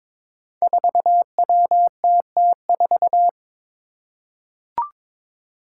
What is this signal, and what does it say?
Morse code, heard as a keyed tone, about 23 wpm, 700 Hz. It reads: 4WTT4
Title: Five Challenging Letter-Number Combinations 22wpm